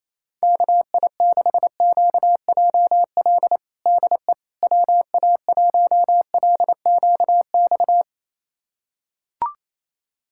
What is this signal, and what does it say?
Morse code, heard as a keyed tone, about 28 wpm, 700 Hz. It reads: KI6QJL DE WA1LQX
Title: USA Callsign DE Callsign - 6 Characters - 28wpm